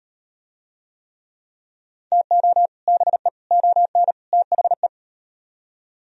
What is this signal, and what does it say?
Morse code, heard as a keyed tone, about 38 wpm, 700 Hz. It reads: TO BE ON THE